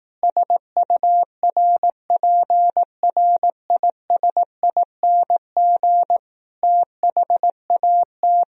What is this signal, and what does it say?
Morse code, heard as a keyed tone, about 18 wpm, 700 Hz. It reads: SURPRISING THAT
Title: Speed-Racing - Sets of 5 Words 15wpm